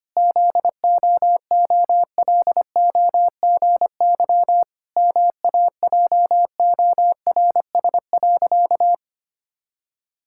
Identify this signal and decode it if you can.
Morse code, heard as a keyed tone, about 25 wpm, 700 Hz. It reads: ZOOLOGY MAJORS.
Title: Pangram Challenge v2 - 25wpm